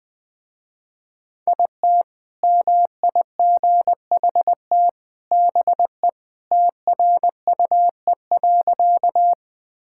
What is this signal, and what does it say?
Morse code, heard as a keyed tone, about 20 wpm, 700 Hz. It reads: IT MIGHT BE TRUE.
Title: Sentences with 2-Word Phrases - Spoken Beginning - 20wpm